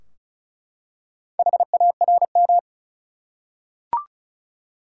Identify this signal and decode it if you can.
Morse code, heard as a keyed tone, about 34 wpm, 700 Hz. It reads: HARM
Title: Four Letter Words - 35wpm